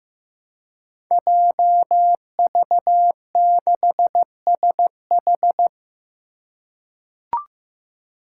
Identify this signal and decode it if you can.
Morse code, heard as a keyed tone, about 15 wpm, 700 Hz. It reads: JV6SH
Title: Five Challenging Letter-Number Combinations 15wpm